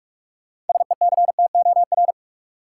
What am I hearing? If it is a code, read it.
Morse code, heard as a keyed tone, about 42 wpm, 700 Hz. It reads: SECTOR
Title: Six Letter Words - 45wpm